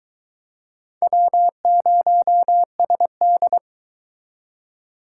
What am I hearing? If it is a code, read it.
Morse code, heard as a keyed tone, about 22 wpm, 700 Hz. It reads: W0SD